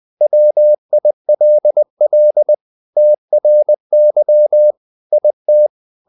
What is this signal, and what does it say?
Morse code, heard as a keyed tone, about 20 wpm, 600 Hz. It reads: WILL TRY IT
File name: the old man and the sea part 2 20wpm